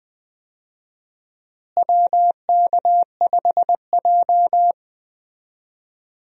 Morse code, heard as a keyed tone, about 20 wpm, 700 Hz. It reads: WK5J
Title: USA Call Signs - 4 Characters - Repeated 3x - 20wpm